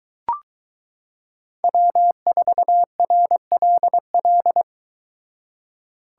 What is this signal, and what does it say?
Morse code, heard as a keyed tone, about 22 wpm, 700 Hz. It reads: W4RLL